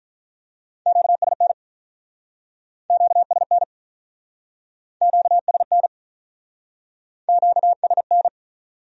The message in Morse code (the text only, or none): QSN QSN QSN QSN